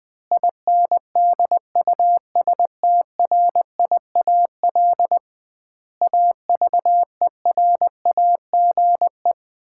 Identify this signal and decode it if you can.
Morse code, heard as a keyed tone, about 20 wpm, 700 Hz. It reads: INDUSTRIAL AVERAGE
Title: 2x Word-Spacing - Sets of 5 Words - 20wpm